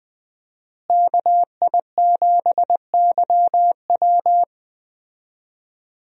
Morse code, heard as a keyed tone, about 20 wpm, 700 Hz. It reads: KI7YW
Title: USA Call Signs - 5 Characters - Repeated 3x - 20wpm